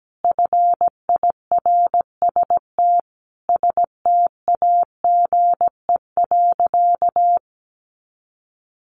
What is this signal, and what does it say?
Morse code, heard as a keyed tone, about 17 wpm, 700 Hz. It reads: FIRST STAGE.
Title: Sentences from Top 1200 Words 17wpm